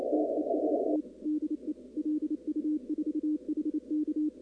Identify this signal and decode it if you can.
Morse code, heard as a keyed tone, about 28 wpm, 300 Hz. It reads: TEST DE LU4HK
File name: contests / 2019-cw-wpx-kt5j-k5pi- -2330 0-23